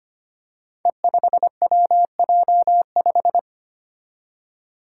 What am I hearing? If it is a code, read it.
Morse code, heard as a keyed tone, about 25 wpm, 700 Hz. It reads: E5WJ5
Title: Five Challenging Letter-Number Combinations 25wpm